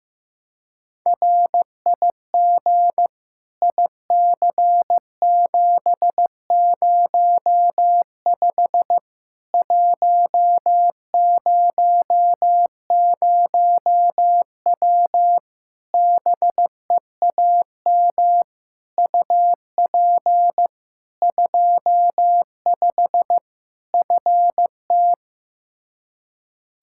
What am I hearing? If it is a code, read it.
Morse code, heard as a keyed tone, about 15 wpm, 700 Hz. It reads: RIG IC705 100W BEAM UP 25 FT
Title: K1USN SST QSO Practice 15wpm